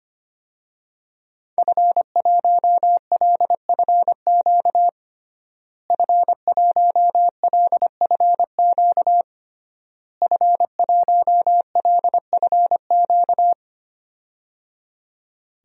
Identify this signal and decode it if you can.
Morse code, heard as a keyed tone, about 25 wpm, 700 Hz. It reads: F1LFQ F1LFQ F1LFQ
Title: DXCC-100 Call Signs - 5 Characters - Repeated 3x - 25wpm